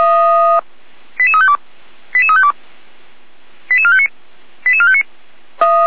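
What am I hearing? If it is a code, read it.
Five-tone selective call (ZVEI), heard as quick runs of five tones, tones of about 70 ms. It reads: 88352 88352 88459 88459
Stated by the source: Five-tone ZVEI selective call received by SDR, Feuerwehr Orlamünde.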